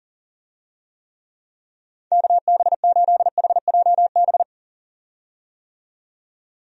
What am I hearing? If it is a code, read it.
Morse code, heard as a keyed tone, about 38 wpm, 700 Hz. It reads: KB8HJB
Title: USA Call Signs - 6 Characters - 40wpm